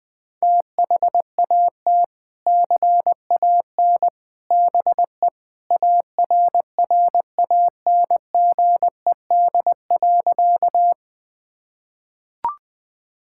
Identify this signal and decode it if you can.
Morse code, heard as a keyed tone, about 21 wpm, 700 Hz. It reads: THAT CAN BE ARRANGED.
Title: Sentences with 3-Word Phrases - 20wpm